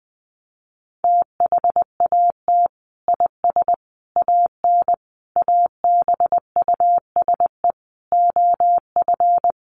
Morse code, heard as a keyed tone, about 20 wpm, 700 Hz. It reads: THAT IS AN ABUSE OF
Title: Sentences from Top 1300 Words 20wpm